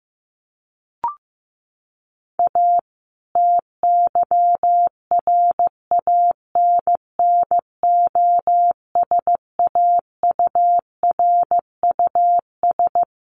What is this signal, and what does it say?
Morse code, heard as a keyed tone, about 15 wpm, 700 Hz. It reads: A TYRANNOSAURUS
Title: A Sound Of Thunder 15wpm